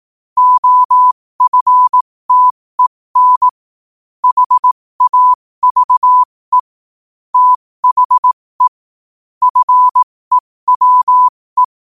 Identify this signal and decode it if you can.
Morse code, heard as a keyed tone, about 18 wpm, spaced as for 15 wpm, 1000 Hz. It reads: OFTEN HAVE THE FEWE
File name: qotdfiles / 15wpm